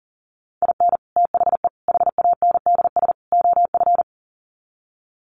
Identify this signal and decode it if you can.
Morse code, heard as a keyed tone, about 40 wpm, 700 Hz. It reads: IN THE HANDS OF
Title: Sets of 4 Words 40wpm